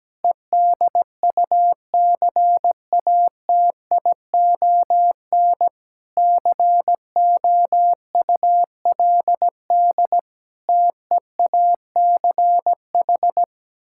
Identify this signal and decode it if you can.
Morse code, heard as a keyed tone, about 17 wpm, 700 Hz. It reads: EDUCATION COULD TEACH
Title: Sentences from Top 1000 Words - Review Part 6 - 17wpm